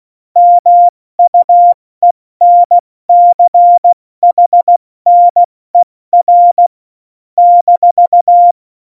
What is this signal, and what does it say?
Morse code, heard as a keyed tone, about 16 wpm, spaced as for 14 wpm, 700 Hz. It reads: MUENCHNER -
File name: Morsekurs / Witze / Witz 10.Dezember 2019, 70 F80